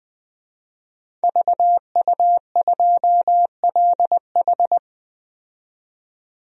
Morse code, heard as a keyed tone, about 20 wpm, 700 Hz. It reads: VU2LH
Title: DXCC-100 Call Signs - 5 Characters - 20wpm